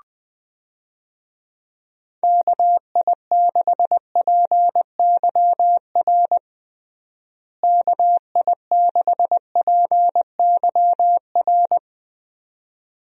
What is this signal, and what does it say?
Morse code, heard as a keyed tone, about 21 wpm, 700 Hz. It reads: KI6PYR KI6PYR
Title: USA Call Signs - 6 Characters - Repeated 2x - 20wpm